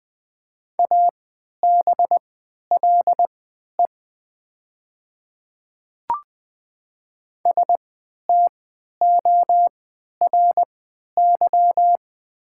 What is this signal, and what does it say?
Morse code, heard as a keyed tone, about 20 wpm, spaced as for 11 wpm, 700 Hz. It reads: ABLE STORY
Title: 3x Character Spacing - Top 500 Words - 20wpm